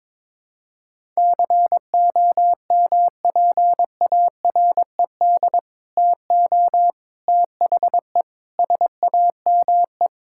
Morse code, heard as a keyed tone, about 22 wpm, 700 Hz. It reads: COMPARED TO THE SAME
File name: practice / 22wpm / Sets of 7 Words - 22wpm